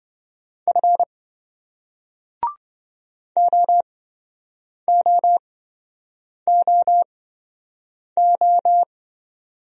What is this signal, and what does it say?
Morse code, heard as a keyed tone, about 27 wpm, spaced as for 7 wpm, 700 Hz. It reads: F OOOO